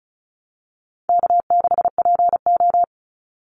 Morse code, heard as a keyed tone, about 34 wpm, 700 Hz. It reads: K6PO
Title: Call Signs Repeated 3x - 35wpm